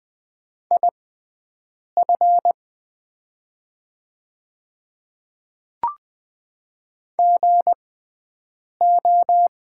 Morse code, heard as a keyed tone, about 20 wpm, spaced as for 7 wpm, 700 Hz. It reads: IF GO
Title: 6x Character Spacing - Top 100 Words - 20wpm